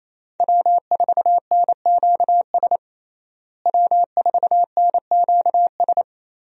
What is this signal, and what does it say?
Morse code, heard as a keyed tone, about 28 wpm, 700 Hz. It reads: W4NQS W4NQS
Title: USA Call Signs - 5 Characters - Repeated 2x - 28wpm